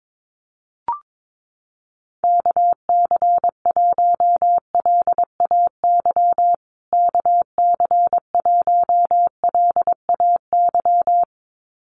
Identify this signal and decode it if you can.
Morse code, heard as a keyed tone, about 22 wpm, 700 Hz. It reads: KC1LAY KC1LAY
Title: USA Call Signs Repeated 2x 22wpm